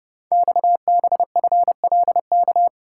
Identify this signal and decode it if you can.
Morse code, heard as a keyed tone, about 30 wpm, 700 Hz. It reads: XBFLK
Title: Code Groups - 5 Letters - 30wpm